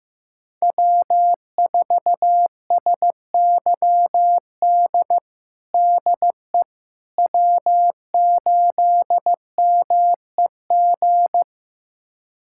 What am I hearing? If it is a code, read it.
Morse code, heard as a keyed tone, about 15 wpm, 700 Hz. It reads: W4SYD DE W8MEG
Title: USA Callsign DE Callsign - 5 Characters - 15wpm